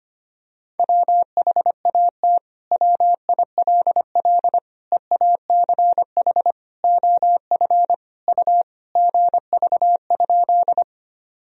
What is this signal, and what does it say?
Morse code, heard as a keyed tone, about 25 wpm, 700 Hz. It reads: WHAT WILL EACH OF U GV?
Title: Sentences from Top 200 Words and Abbreviations 25wpm